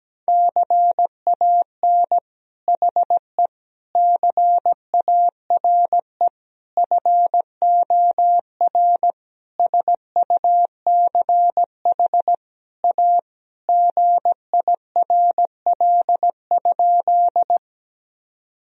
Morse code, heard as a keyed tone, about 17 wpm, 700 Hz. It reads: CAN HE CARE FOR SUCH A GIRL?
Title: Sentences from Top 1000 Words - Review Part 1 - 17wpm